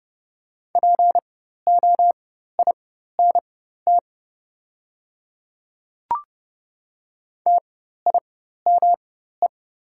Morse code, heard as a keyed tone, about 29 wpm, spaced as for 14 wpm, 690 Hz. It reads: POINT TIME
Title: 4x Character Spacing - Top 200 Words - 30wpm